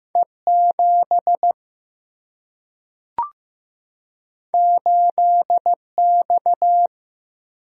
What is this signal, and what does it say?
Morse code, heard as a keyed tone, about 15 wpm, 700 Hz. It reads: E 7 8 X
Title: Single-Letter Single-Number 15wpm